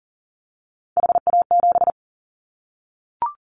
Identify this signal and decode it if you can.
Morse code, heard as a keyed tone, about 40 wpm, 700 Hz. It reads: HA7
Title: Callsign Prefixes for Top 25 DXCC - Repeated 3x - 40wpm